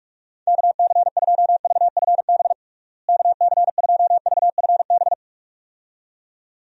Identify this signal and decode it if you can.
Morse code, heard as a keyed tone, about 46 wpm, 700 Hz. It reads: KC2VFB KC2VFB
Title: USA Call Signs - 6 Characters - Repeated 2x - 45wpm